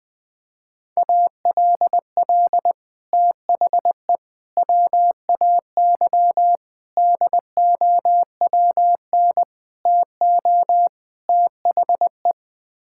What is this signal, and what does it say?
Morse code, heard as a keyed tone, about 20 wpm, 700 Hz. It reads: ALL THE WAY DOWN TO THE
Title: Sets of 6 Words - 20wpm